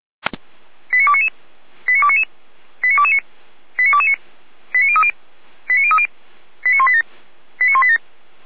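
Five-tone selective call (ZVEI), heard as quick runs of five tones, tones of about 70 ms. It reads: 89299 89299 89229 89229 89030 89030 89187 89187
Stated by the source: ZVEI five-tone selective call 89240 received by SDR, from fire brigade Orlamünde (feuerwehr-orlamuende) server.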